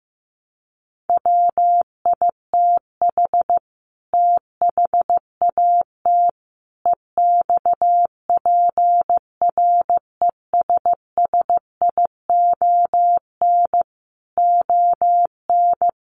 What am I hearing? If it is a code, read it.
Morse code, heard as a keyed tone, about 15 wpm, 700 Hz. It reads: WITH THAT EXPRESSION ON
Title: Sentences from Top 1200 Words 15wpm